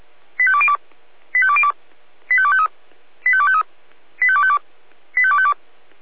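Five-tone selective call (ZVEI), heard as quick runs of five tones, tones of about 70 ms. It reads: 85202 85202 85253 85253 85252 85252
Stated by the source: Five-tone ZVEI selective call received on SDR.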